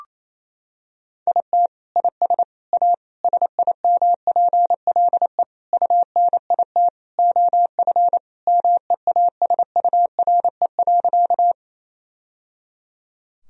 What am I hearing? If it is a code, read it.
Morse code, heard as a keyed tone, about 28 wpm, 700 Hz. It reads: IT IS A SIMPLE UNIT OF MEASURE.
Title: Sentences from Top 1000 Words - Review Part 2 - 28wpm